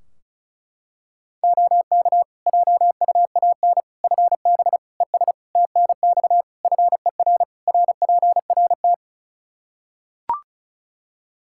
Morse code, heard as a keyed tone, about 36 wpm, 700 Hz. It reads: OK JUAN FB ES TNX FER RPRT
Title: K1USN SST QSO Practice 35wpm